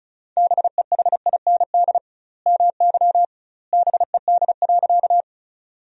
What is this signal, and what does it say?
Morse code, heard as a keyed tone, about 36 wpm, 700 Hz. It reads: BEHIND MY BED.
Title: 2x Word-Spacing - Sentences from Top 500 Words - 35wpm